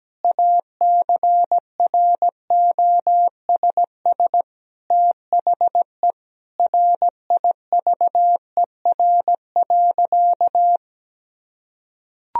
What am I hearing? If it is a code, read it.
Morse code, heard as a keyed tone, about 17 wpm, 700 Hz. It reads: ACROSS THE RIVER.
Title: Sentences from Top 1000 Words - Review Part 2 - 17wpm